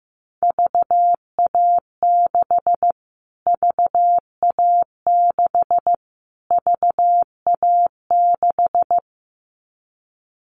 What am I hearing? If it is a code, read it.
Morse code, heard as a keyed tone, about 15 wpm, 700 Hz. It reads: VA6 VA6 VA6
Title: Callsign Prefixes for Top 25 DXCC - Repeated 3x - 15wpm